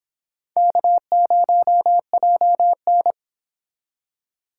Morse code, heard as a keyed tone, about 31 wpm, 700 Hz. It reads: K0JN